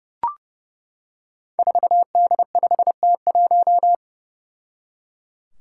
Morse code, heard as a keyed tone, about 30 wpm, 700 Hz. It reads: 4D5T1
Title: Five Challenging Letter-Number Combinations 30wpm